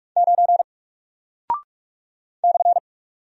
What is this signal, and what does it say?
Morse code, heard as a keyed tone, about 44 wpm, 700 Hz. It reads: 9 /
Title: Single Character (Letter:Number:Punctuation:Prosign) 45wpm